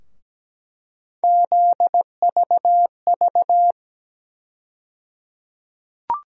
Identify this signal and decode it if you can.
Morse code, heard as a keyed tone, about 17 wpm, 700 Hz. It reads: ZVV
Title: USA Call Sign Postfixes - 3 Letters 17wpm